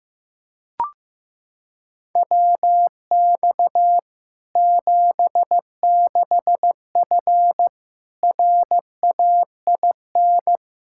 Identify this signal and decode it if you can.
Morse code, heard as a keyed tone, about 15 wpm, 700 Hz. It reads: WX 76F RAIN
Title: Practice QSO - Exchange 2 - Info - 15wpm